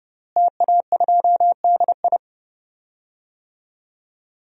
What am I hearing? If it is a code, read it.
Morse code, heard as a keyed tone, about 32 wpm, 700 Hz. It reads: TA2DI
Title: DXCC-100 Call Signs - 5 Characters - Repeated 2x - 30wpm